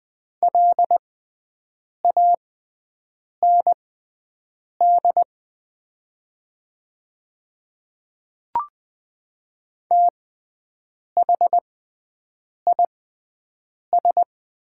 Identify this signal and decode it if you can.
Morse code, heard as a keyed tone, about 20 wpm, spaced as for 7 wpm, 700 Hz. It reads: LAND THIS
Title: 6x Character Spacing - Top 200 Words - 20wpm